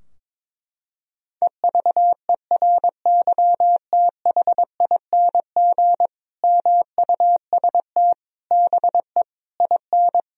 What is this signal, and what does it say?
Morse code, heard as a keyed tone, about 22 wpm, 700 Hz. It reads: EVERYTHING MUST BE IN
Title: Sentences from Top 1000 Words - Review Part 2 - 22wpm